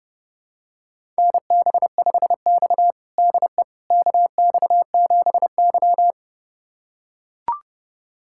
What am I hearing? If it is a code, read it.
Morse code, heard as a keyed tone, about 32 wpm, 700 Hz. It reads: NB5X DE KX7Y
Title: USA Callsign DE Callsign - 4 Characters - 30wpm